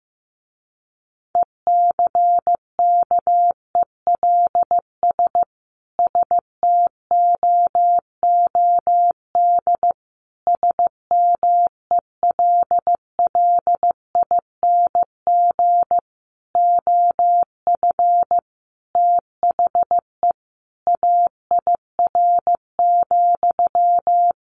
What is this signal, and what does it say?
Morse code, heard as a keyed tone, about 15 wpm, 700 Hz. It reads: ECKELS STOOD SMELLING OF THE AIR,
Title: A Sound Of Thunder 15wpm